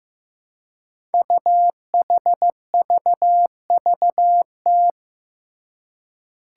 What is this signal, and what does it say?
Morse code, heard as a keyed tone, about 15 wpm, 700 Hz. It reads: UHVVT
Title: Five Challenging Letter-Number Combinations 15wpm